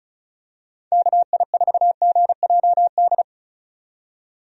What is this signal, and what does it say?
Morse code, heard as a keyed tone, about 35 wpm, 700 Hz. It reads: KI4GJD
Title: USA Call Signs - 6 Characters - 35wpm